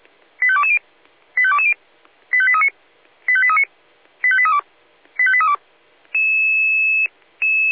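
Five-tone selective call (ZVEI), heard as quick runs of five tones, tones of about 70 ms. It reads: 85220 85220 85839 85839 85832 85832
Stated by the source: Five-tone ZVEI selective call received on an SDR.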